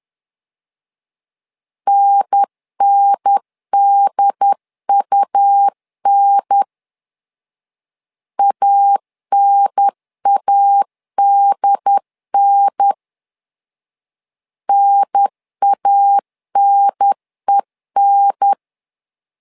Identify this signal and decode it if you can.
Morse code, heard as a keyed tone, about 10 wpm, 800 Hz. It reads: NNDUN ANADN NANEN
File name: LEZIONI / 2 UHDN / 2 UHDN 10wpm 1904